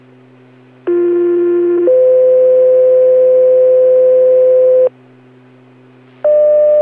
Two-tone page, heard as a one-second tone, then a three-second tone, about 350 then 510 Hz. It is heard from 0.9 to 4.9 s.